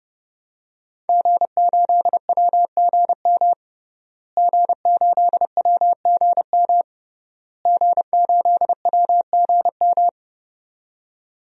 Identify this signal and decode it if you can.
Morse code, heard as a keyed tone, about 30 wpm, 700 Hz. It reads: G8WGM G8WGM G8WGM
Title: DXCC-100 Call Signs - 5 Characters - Repeated 3x - 30wpm